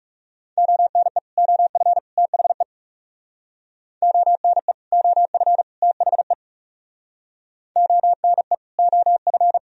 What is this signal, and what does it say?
Morse code, heard as a keyed tone, about 38 wpm, 690 Hz. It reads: ONE OF THE ONE OF THE ONE OF